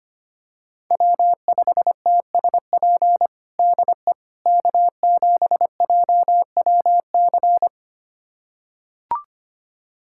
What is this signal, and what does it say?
Morse code, heard as a keyed tone, about 25 wpm, 700 Hz. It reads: W5TSP DE K7JWC
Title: USA Callsign DE Callsign - 5 Characters - 25wpm